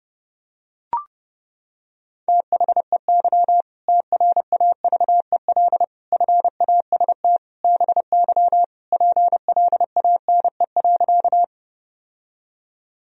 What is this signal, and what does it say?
Morse code, heard as a keyed tone, about 30 wpm, 700 Hz. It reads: THEY TRAVEL FAST BY PLANE.
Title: Sentences from Top 1000 Words - Review Part 3 - 30wpm